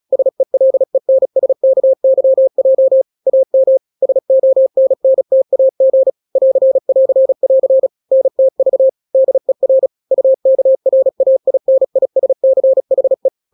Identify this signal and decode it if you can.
Morse code, heard as a keyed tone, about 36 wpm, 530 Hz. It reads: SELENSKYJ AM SONNTAG <AR><AR><AR> NTV DER UKRAINISCHE
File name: daily / 35wpm